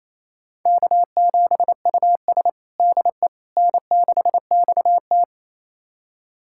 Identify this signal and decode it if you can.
Morse code, heard as a keyed tone, about 27 wpm, 700 Hz. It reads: K7US DE N6XT
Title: USA Callsign DE Callsign - 4 Characters - 28wpm